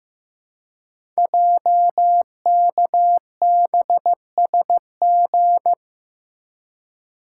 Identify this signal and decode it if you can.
Morse code, heard as a keyed tone, about 15 wpm, 700 Hz. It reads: JKBSG
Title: Code Groups - 5 Letters - 15wpm